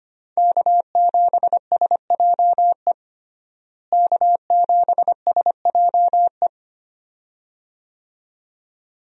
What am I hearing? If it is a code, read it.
Morse code, heard as a keyed tone, about 25 wpm, 700 Hz. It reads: K7SJE K7SJE
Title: USA Call Signs - 5 Characters - Repeated 2x - 25wpm